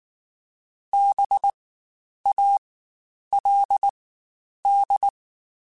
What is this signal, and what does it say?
Morse code, heard as a keyed tone, about 19 wpm, spaced as for 9 wpm, 800 Hz. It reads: BALD
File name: Spaced Alphas / Spaced Alphas mp3s / Bs sp @18